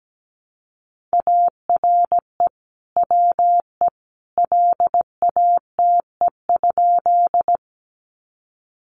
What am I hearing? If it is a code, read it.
Morse code, heard as a keyed tone, about 17 wpm, 700 Hz. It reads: ARE WE LATE?
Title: Sentences from Top 300 Words 17wpm